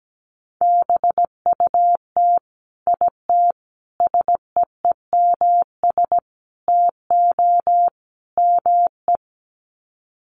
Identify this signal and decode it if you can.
Morse code, heard as a keyed tone, about 17 wpm, 700 Hz. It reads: BUT IT SEEMS TO ME
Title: Sets of 5 Words Encore2 17wpm